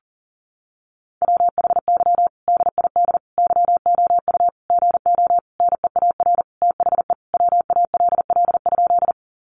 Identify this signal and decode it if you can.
Morse code, heard as a keyed tone, about 40 wpm, 700 Hz. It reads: WHY DID YOU GO NEAR THE WALL?
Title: Sentences from Top 1100 Words 40wpm